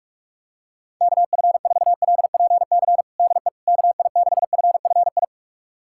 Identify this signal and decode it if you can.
Morse code, heard as a keyed tone, about 46 wpm, 700 Hz. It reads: KF4LPC DE KI6FFI
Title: USA Callsign DE Callsign - 6 Characters - 45wpm